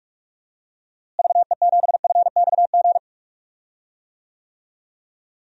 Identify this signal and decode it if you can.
Morse code, heard as a keyed tone, about 43 wpm, 700 Hz. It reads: VE7FXG
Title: DXCC-100 Call Signs - 6 Characters - Repeated 2x - 45wpm